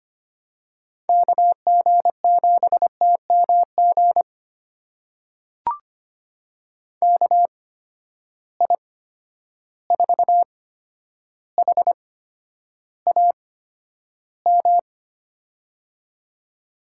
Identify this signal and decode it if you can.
Morse code, heard as a keyed tone, about 25 wpm, spaced as for 11 wpm, 700 Hz. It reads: KG7TMG KI4HAM
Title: USA Call Signs - 6 Characters - 8x Character Spacing - 25wpm